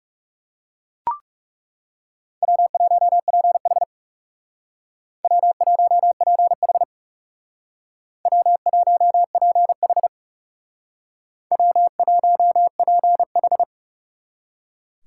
Morse code, heard as a keyed tone, about 38 wpm, 700 Hz. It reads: W1PH W1PH W1PH W1PH